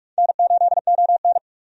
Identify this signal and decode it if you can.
Morse code, heard as a keyed tone, about 41 wpm, 700 Hz. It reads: N8ON
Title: USA Call Signs - 4 Characters - 45wpm